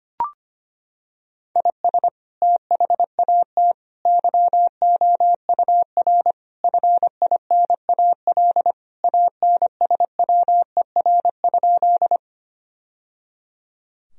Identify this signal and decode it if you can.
Morse code, heard as a keyed tone, about 25 wpm, 700 Hz. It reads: IS THAT YOUR FINAL ANSWER?
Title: Sentences from Top 1000 Words - Review Part 2 - 25wpm